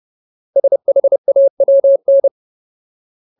Morse code, heard as a keyed tone, about 30 wpm, 550 Hz. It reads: SHAWN
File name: Ultimate Word List / 30WPM Ultimate